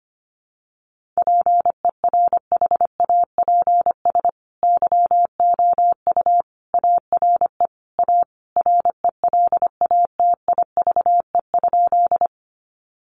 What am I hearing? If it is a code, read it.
Morse code, heard as a keyed tone, about 25 wpm, 700 Hz. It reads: PERHAPS YOU ARE A RELATIVE?
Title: Sentences from Top 1400 Words 25wpm